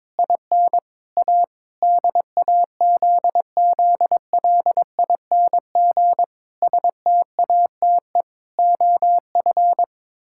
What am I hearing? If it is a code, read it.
Morse code, heard as a keyed tone, about 21 wpm, 700 Hz. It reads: IN A DAZZLING STATE OF
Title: Pangram Challenge v2 - 22wpm